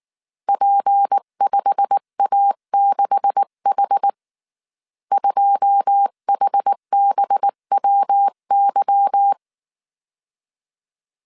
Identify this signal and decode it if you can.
Morse code, heard as a keyed tone, about 19 wpm, 800 Hz. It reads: P5A6H 2HBWY